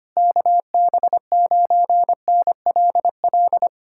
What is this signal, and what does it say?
Morse code, heard as a keyed tone, about 24 wpm, 700 Hz. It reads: KB9NLL
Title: Speed-Racing Common QSO Phrases - 25wpm